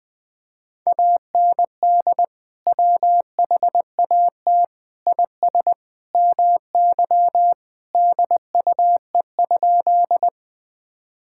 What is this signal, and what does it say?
Morse code, heard as a keyed tone, about 20 wpm, 700 Hz. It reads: AND WHAT IS MY DUE?
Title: Sentences from Top 1000 Words - Review Part 6 - 20wpm